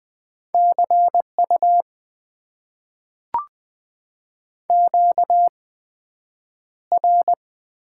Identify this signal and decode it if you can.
Morse code, heard as a keyed tone, about 20 wpm, spaced as for 9 wpm, 700 Hz. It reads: CU QR
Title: Top 90 QSO Elements - 8x Character Spacing - 20wpm